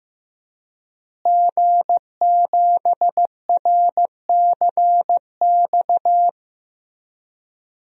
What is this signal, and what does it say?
Morse code, heard as a keyed tone, about 15 wpm, 700 Hz. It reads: G7RCX
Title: DXCC-100 Call Signs - 5 Characters - Repeated 3x - 15wpm